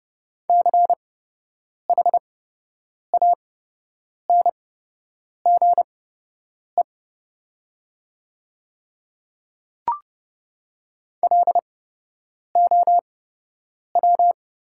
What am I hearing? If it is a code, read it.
Morse code, heard as a keyed tone, about 29 wpm, spaced as for 8 wpm, 700 Hz. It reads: CHANGE LOW
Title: 8x Character Spacing - Top 200 Words - 30wpm